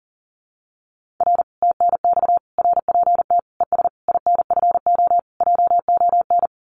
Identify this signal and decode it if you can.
Morse code, heard as a keyed tone, about 37 wpm, 700 Hz. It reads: R TNX RPT ES INFO JON
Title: Common QSO Phrases Encore 40wpm